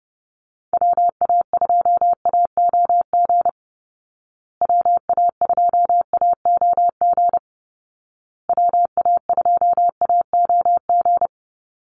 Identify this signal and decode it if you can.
Morse code, heard as a keyed tone, about 30 wpm, 700 Hz. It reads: WA2AOG WA2AOG WA2AOG
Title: Call Signs Repeated 2x - 30wpm